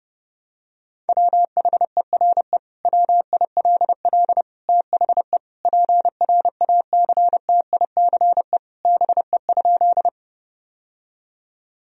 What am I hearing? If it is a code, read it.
Morse code, heard as a keyed tone, about 30 wpm, 700 Hz. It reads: WHERE WILL THE PRACTICE BE?
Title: Sentences from Top 1000 Words - Review Part 4 - 30wpm